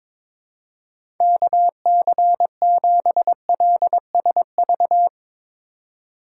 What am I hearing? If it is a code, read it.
Morse code, heard as a keyed tone, about 21 wpm, 700 Hz. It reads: KC7LSV
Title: USA Call Signs - 6 Characters - 2x Character Spacing - 22wpm